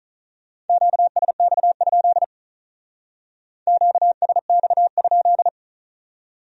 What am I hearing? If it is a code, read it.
Morse code, heard as a keyed tone, about 42 wpm, 700 Hz. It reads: QSX? QSX?
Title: Speed-Racing Q-Codes - 35wpm